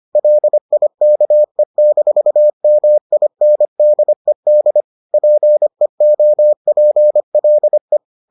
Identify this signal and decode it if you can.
Morse code, heard as a keyed tone, about 25 wpm, 600 Hz. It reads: LIKE-MINDED PEOPLE